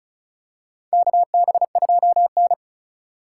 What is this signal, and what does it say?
Morse code, heard as a keyed tone, about 36 wpm, 700 Hz. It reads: KB2N
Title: USA Call Signs - 4 Characters - 3x Character Spacing - 35wpm